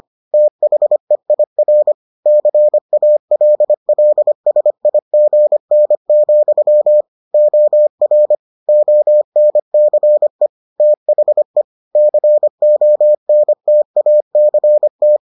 Morse code, heard as a keyed tone, about 25 wpm, 600 Hz. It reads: THEIR CALLSIGN, OR ONCE THE CONTACT